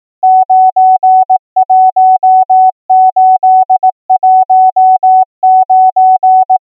Morse code, heard as a keyed tone, about 18 wpm, 750 Hz. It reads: 91819
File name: numbers / V2-morse